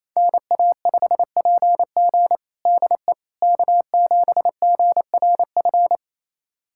Morse code, heard as a keyed tone, about 28 wpm, 700 Hz. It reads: NA5PG DE K7GRF
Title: USA Callsign DE Callsign - 5 Characters - 28wpm